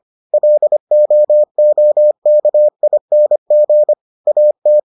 Morse code, heard as a keyed tone, about 25 wpm, 600 Hz. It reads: LOOKING AT